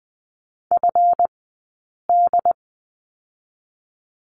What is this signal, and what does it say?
Morse code, heard as a keyed tone, about 20 wpm, 700 Hz. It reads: F D
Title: 137 Lesson 23d - Letter f - Two Characters 20wpm